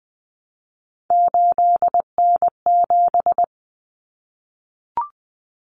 Morse code, heard as a keyed tone, about 20 wpm, 700 Hz. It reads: 8N7
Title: Common Cut Numbers - 3-Digit 20wpm